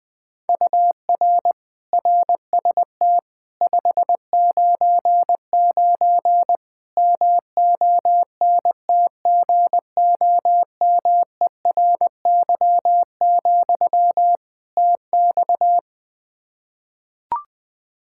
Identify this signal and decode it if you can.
Morse code, heard as a keyed tone, about 21 wpm, 700 Hz. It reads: UR RST 599 MONTGOMERY, TX
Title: K1USN SST QSO Practice 20wpm